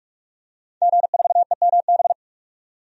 Morse code, heard as a keyed tone, about 41 wpm, 700 Hz. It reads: G4EMB
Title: DXCC-100 Call Signs - 5 Characters - 45wpm